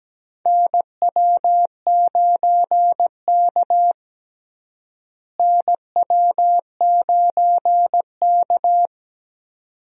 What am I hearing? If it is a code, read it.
Morse code, heard as a keyed tone, about 17 wpm, 700 Hz. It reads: NW9K NW9K
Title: USA Call Signs - 4 Characters - Repeated 2x - 17wpm